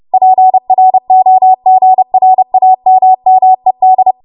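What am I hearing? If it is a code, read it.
Morse code, heard as a keyed tone, about 30 wpm, 750 Hz. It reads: PROGRAMMED
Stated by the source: Morse practice transmission at 30 WPM.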